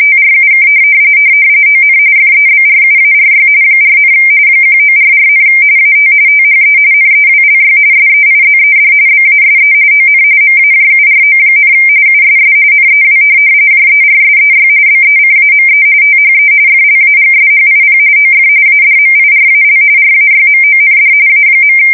RTTY, about 45.45 baud. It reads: TRUK ON FEBRUARY 4, 1944, AT THE NEWLY SEIZED BASE ON MAJURO, USS NEW JERSEY BECAME THE FLAGSHIP OF ADMIRAL RAYMOND A.
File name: fsk170 / USS-New-Jersey